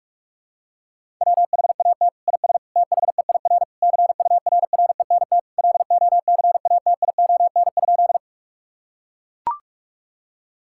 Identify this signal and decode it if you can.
Morse code, heard as a keyed tone, about 44 wpm, 700 Hz. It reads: WHAT IS THEIR CURRENT LOCATION?
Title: Sentences from Top 1000 Words - Review Part 6 - 45wpm